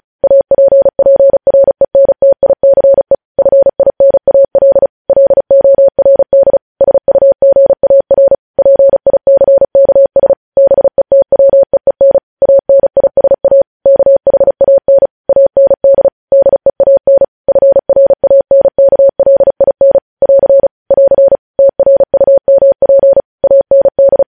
Morse code, heard as a keyed tone, about 36 wpm, 550 Hz. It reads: APPRENTICE FINAL LORD SUGAR PICKS BETWEEN ANISA KHAN AND DEAN FRANKLIN <AR> <AR> TRUMP AND
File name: podcast / News headlines-1x-35x35